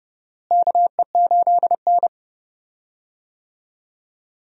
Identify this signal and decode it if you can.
Morse code, heard as a keyed tone, about 30 wpm, 700 Hz. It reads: KE8N
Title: USA Call Signs - 4 Characters - Repeated 3x - 30wpm